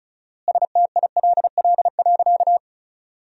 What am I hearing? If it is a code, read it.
Morse code, heard as a keyed tone, about 36 wpm, 700 Hz. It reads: STILL.
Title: 4x Word-Spacing - Sentences from Top 500 Words - 35wpm